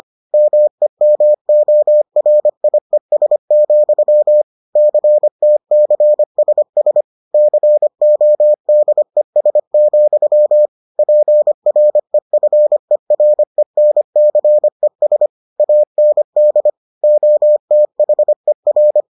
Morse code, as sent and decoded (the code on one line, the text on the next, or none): -- . -- --- .-. .. ... --..-- / -.-. - -.-. ... ... / -.-. --- -.. ... --..-- / .--. .-. ..-. .-. -. -.-. ... / .- -. -.. / --- - .... .-.
MEMORIES, CTCSS CODES, PREFERENCES AND OTHER